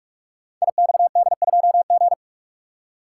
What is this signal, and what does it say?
Morse code, heard as a keyed tone, about 44 wpm, 700 Hz. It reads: IXD2G
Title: Code Groups - 5 Letter:Numbers - 45wpm